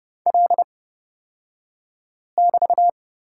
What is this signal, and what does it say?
Morse code, heard as a keyed tone, about 30 wpm, spaced as for 5 wpm, 700 Hz. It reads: L<BT>
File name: Lesson 36d - BT - Two Characters 30wpm